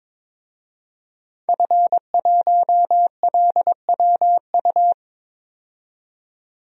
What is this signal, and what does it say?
Morse code, heard as a keyed tone, about 22 wpm, 700 Hz. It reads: F1LWU
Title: DXCC-100 Call Signs - 5 Characters - 22wpm